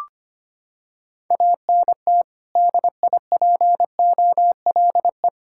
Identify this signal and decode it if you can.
Morse code, heard as a keyed tone, about 25 wpm, 700 Hz. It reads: ANT DIPOLE
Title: Practice QSO - Exchange 2 - Info - 25wpm